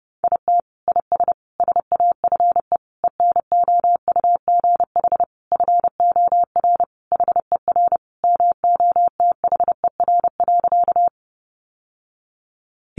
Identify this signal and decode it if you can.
Morse code, heard as a keyed tone, about 28 wpm, 700 Hz. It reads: IT IS SAFE ENOUGH FOR HER MOTHER.
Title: Sentences from Top 900 Words 30wpm